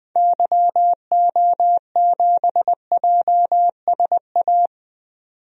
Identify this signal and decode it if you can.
Morse code, heard as a keyed tone, about 20 wpm, 700 Hz. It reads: YO7JSA
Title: DXCC-100 Call Signs - 6 Characters - 20wpm